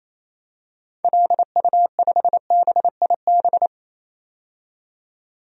Morse code, heard as a keyed tone, about 28 wpm, 700 Hz. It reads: LU5BIB
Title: DXCC-100 Call Signs - 6 Characters - 28wpm